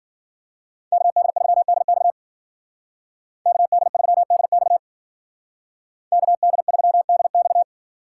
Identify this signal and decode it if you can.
Morse code, heard as a keyed tone, about 48 wpm, 700 Hz. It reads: KD3DX KD3DX KD3DX
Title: Speed Racing - USA 2x2 Call Signs 40wpm